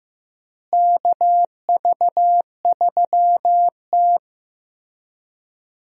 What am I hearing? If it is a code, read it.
Morse code, heard as a keyed tone, about 15 wpm, 700 Hz. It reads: KV3T
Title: USA Call Signs - 4 Characters - Repeated 2x - 15wpm